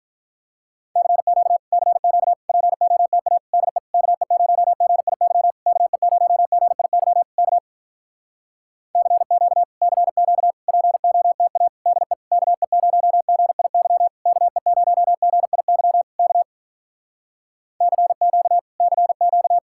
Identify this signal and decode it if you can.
Morse code, heard as a keyed tone, about 46 wpm, 700 Hz. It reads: CQ CQ POTA DE KE0GIY KE0GIY K CQ CQ POTA DE KE0GIY KE0GIY K CQ CQ